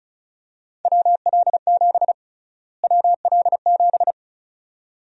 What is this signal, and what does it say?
Morse code, heard as a keyed tone, about 36 wpm, 700 Hz. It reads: WL7 WL7
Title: USA Call Sign Prefixes - Repeated 2x - 35wpm